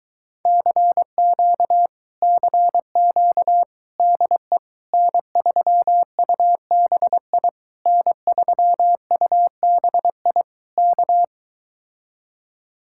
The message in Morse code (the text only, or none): CQ CQ DE N3UBI N3UBI K